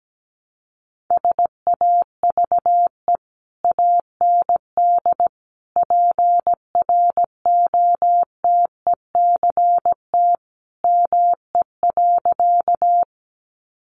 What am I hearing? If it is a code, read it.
Morse code, heard as a keyed tone, about 17 wpm, 700 Hz. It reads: SAVE AND PROTECT ME.
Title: Sentences from Top 1100 Words 17wpm